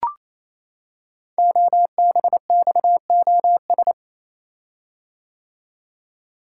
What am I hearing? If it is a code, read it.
Morse code, heard as a keyed tone, about 27 wpm, 700 Hz. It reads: OBXOS
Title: Code Groups - 5 Letters - 28wpm